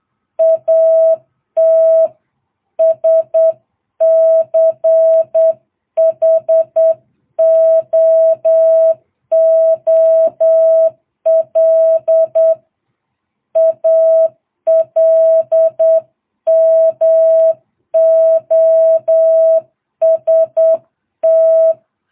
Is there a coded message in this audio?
Yes: Morse code from 0.4 to 21.7 s, heard as a keyed tone, about 8 wpm, 660 Hz. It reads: AT SCHOOL ALMOST